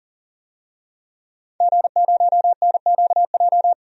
Morse code, heard as a keyed tone, about 40 wpm, 700 Hz. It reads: G0NQJ
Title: DXCC-100 Call Signs - 5 Characters - 40wpm